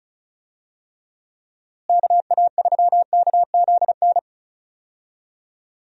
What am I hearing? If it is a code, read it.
Morse code, heard as a keyed tone, about 34 wpm, 700 Hz. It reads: KA3KZN
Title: USA Call Signs - 6 Characters - Repeated 3x - 35wpm